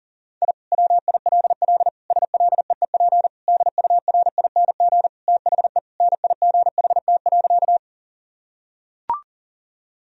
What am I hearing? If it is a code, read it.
Morse code, heard as a keyed tone, about 40 wpm, 700 Hz. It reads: I WILL SLEEP DURING THE NIGHT.
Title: Sentences from Top 1000 Words - Review Part 2 - 40wpm